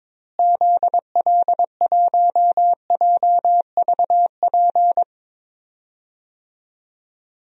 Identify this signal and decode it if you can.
Morse code, heard as a keyed tone, about 22 wpm, 700 Hz. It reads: ZL1JVP
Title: DXCC-100 Call Signs - 6 Characters - Repeated 3x - 22wpm